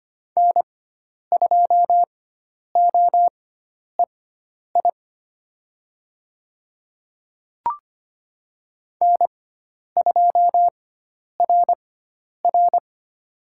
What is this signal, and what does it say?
Morse code, heard as a keyed tone, about 25 wpm, spaced as for 10 wpm, 700 Hz. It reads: N2OEI N2RR
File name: Lesson 16g - Letter c - Call Signs 25wpm